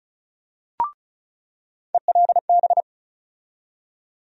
Morse code, heard as a keyed tone, about 36 wpm, 700 Hz. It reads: ELB
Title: USA Call Sign Postfixes - 3 Letters 35wpm